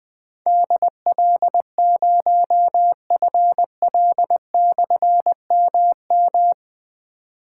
Morse code, heard as a keyed tone, about 21 wpm, 700 Hz. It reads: DL0FL/MM
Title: Difficult Call Signs 20wpm